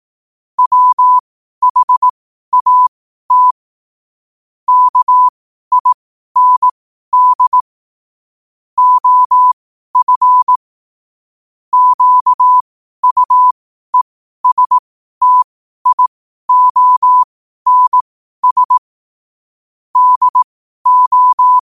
Morse code, heard as a keyed tone, about 18 wpm, spaced as for 12 wpm, 1000 Hz. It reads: WHAT KIND OF QUESTIONS DO
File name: qotdfiles / 12wpm